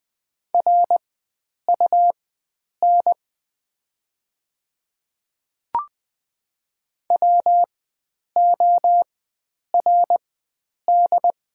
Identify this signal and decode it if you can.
Morse code, heard as a keyed tone, about 20 wpm, spaced as for 9 wpm, 700 Hz. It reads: RUN WORD
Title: 4x Character Spacing - Top 500 Words - 20wpm